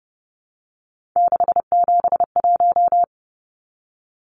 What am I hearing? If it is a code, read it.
Morse code, heard as a keyed tone, about 30 wpm, 700 Hz. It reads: 671
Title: Numbers - 3-Digit 30wpm